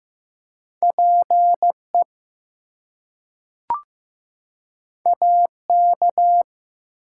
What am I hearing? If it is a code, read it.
Morse code, heard as a keyed tone, about 15 wpm, 700 Hz. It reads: PE AK
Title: US States and Canadian Provinces 15wpm